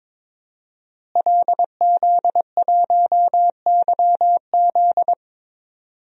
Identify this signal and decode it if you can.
Morse code, heard as a keyed tone, about 22 wpm, 700 Hz. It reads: LZ1YZ
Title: DXCC-100 Call Signs - 5 Characters - Repeated 2x - 22wpm